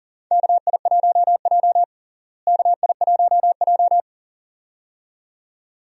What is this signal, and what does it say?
Morse code, heard as a keyed tone, about 37 wpm, 700 Hz. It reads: KI1J KI1J
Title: USA Call Signs - 4 Characters - Repeated 2x - 40wpm